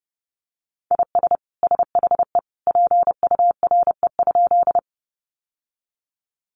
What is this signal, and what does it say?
Morse code, heard as a keyed tone, about 30 wpm, 700 Hz. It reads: IS SHE PURE?
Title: Sentences from Top 1500 Words 30wpm